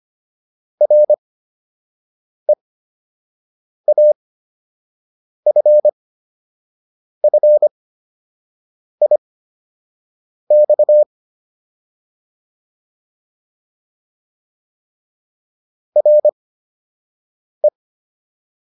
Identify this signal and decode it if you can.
Morse code, heard as a keyed tone, about 25 wpm, spaced as for 6 wpm, 600 Hz. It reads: REAFFIX RE
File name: RE7 06